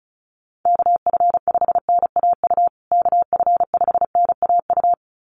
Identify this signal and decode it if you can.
Morse code, heard as a keyed tone, about 34 wpm, 700 Hz. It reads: KF5NAU KF5NAU
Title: USA Call Signs Repeated 2x 35wpm